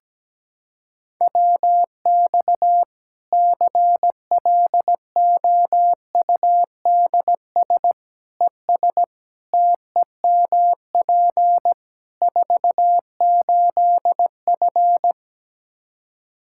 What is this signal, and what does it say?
Morse code, heard as a keyed tone, about 17 wpm, 700 Hz. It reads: WX CLOUDS ES TEMP 48F
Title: Practice QSO - Exchange 2 - Info - 17wpm